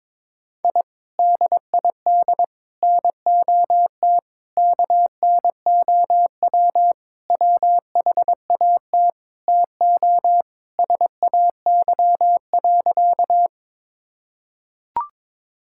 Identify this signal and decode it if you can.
Morse code, heard as a keyed tone, about 22 wpm, 700 Hz. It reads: I DID NOT KNOW WHAT TO SAY.
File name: practice / 22wpm / Sentences with 4-Word Phrases - 22wpm